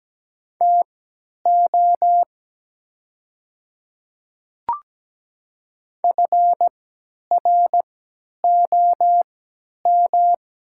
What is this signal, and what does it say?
Morse code, heard as a keyed tone, about 17 wpm, spaced as for 10 wpm, 700 Hz. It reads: TO FROM
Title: 3x Character Spacing - Top 300 Words - 17wpm